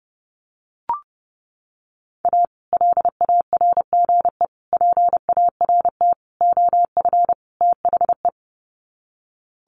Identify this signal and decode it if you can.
Morse code, heard as a keyed tone, about 30 wpm, 700 Hz. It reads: A LARGE PART OF THE
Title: Sets of 5 Words Encore2 30wpm